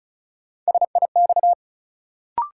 Morse code, heard as a keyed tone, about 36 wpm, 700 Hz. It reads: SIX
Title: Sentences from Top 600 Words - New Word Drill - 35wpm